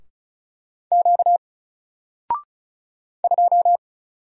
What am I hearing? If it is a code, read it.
Morse code, heard as a keyed tone, about 36 wpm, 700 Hz. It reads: Q 2
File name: Single-Letter-Number-35wpm